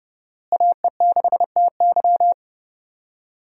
Morse code, heard as a keyed tone, about 32 wpm, 700 Hz. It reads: AE6TY
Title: USA Call Signs - 5 Characters - Repeated 2x - 30wpm